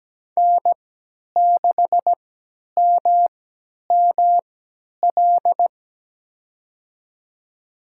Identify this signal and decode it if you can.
Morse code, heard as a keyed tone, about 17 wpm, spaced as for 10 wpm, 700 Hz. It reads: N6MML
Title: USA Call Signs - 5 Characters - 3x Character Spacing - 17wpm